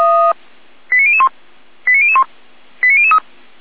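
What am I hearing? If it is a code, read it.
Five-tone selective call (ZVEI), heard as quick runs of five tones, tones of about 70 ms. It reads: 89001 89001 89003
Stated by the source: ZVEI five-tone selective call received by SDR.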